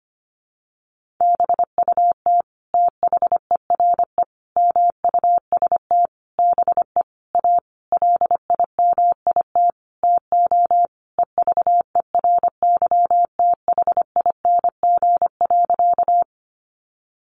Morse code, heard as a keyed tone, about 25 wpm, 700 Hz. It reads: BUT THERE MUST BE A LIMIT TO EVERYTHING.
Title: Sentences from Top 1500 Words 25wpm